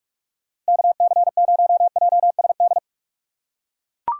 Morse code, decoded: KC0JSD

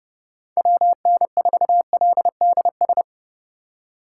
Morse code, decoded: WN4LDS

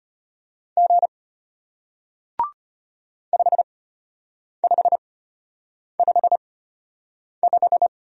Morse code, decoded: G 5555

